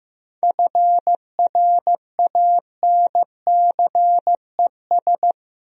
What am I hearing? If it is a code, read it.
Morse code, heard as a keyed tone, about 15 wpm, 700 Hz. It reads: FRANCES
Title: USA Common Names - 15wpm